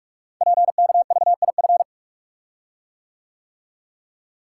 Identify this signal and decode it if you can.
Morse code, heard as a keyed tone, about 44 wpm, 700 Hz. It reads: PKVIF
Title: Code Groups - 5 Letters - 45wpm